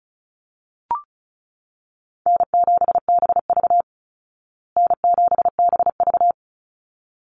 Morse code, heard as a keyed tone, about 34 wpm, 700 Hz. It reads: N7BV N7BV
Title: Easy Call Signs Repeated 2x - 35wpm